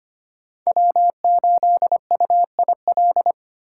Morse code, heard as a keyed tone, about 25 wpm, 700 Hz. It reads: W8UIL
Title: K1USN SST QSO Practice - 25wpm